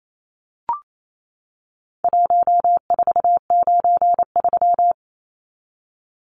Morse code, heard as a keyed tone, about 28 wpm, 700 Hz. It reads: 1493
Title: Numbers - Contest Style - 4-Digit 28wpm